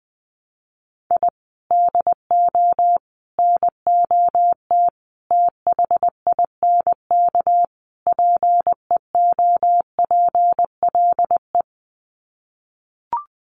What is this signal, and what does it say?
Morse code, heard as a keyed tone, about 20 wpm, 700 Hz. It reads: I DO NOT THINK PEOPLE
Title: Sets of 5 Words Encore2 20wpm